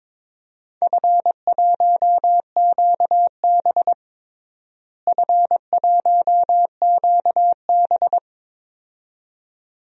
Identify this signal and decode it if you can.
Morse code, heard as a keyed tone, about 22 wpm, 700 Hz. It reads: F1QB F1QB
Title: DXCC-100 Call Signs - 4 Characters - Repeated 2x - 22wpm